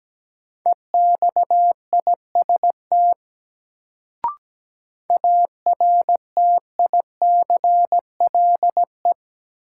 Morse code, heard as a keyed tone, about 17 wpm, 700 Hz. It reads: EXIST ARTICLE
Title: Sentences from Top 1000 Words - Review Part 5 - New Word Drill - 17wpm